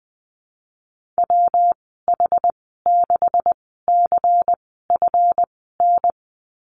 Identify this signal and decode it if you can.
Morse code, heard as a keyed tone, about 20 wpm, spaced as for 15 wpm, 700 Hz. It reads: WH6CFN
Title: 147 Lesson 24g - Letter y - Call Signs 20wpm